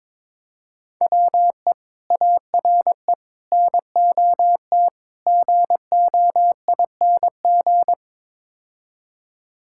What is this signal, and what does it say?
Morse code, heard as a keyed tone, about 22 wpm, 700 Hz. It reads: WE ARE NOT GOING
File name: practice / 22wpm / Build A Phrase - 5-Word Phrases - 22wpm